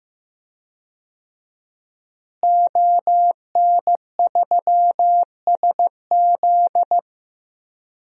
Morse code, heard as a keyed tone, about 15 wpm, 700 Hz. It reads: ON3SZ
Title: DXCC-100 Call Signs - 5 Characters - Repeated 3x - 15wpm